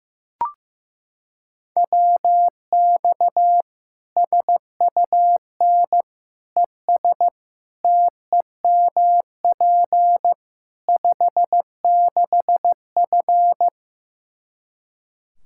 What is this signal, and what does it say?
Morse code, heard as a keyed tone, about 15 wpm, 700 Hz. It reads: WX SUN ES TEMP 56F
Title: Practice QSO - Exchange 2 - Info - 15wpm